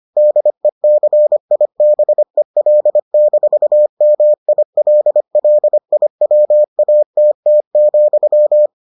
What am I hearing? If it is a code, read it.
Morse code, heard as a keyed tone, about 25 wpm, 600 Hz. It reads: DECIBEL-MILLIWATT,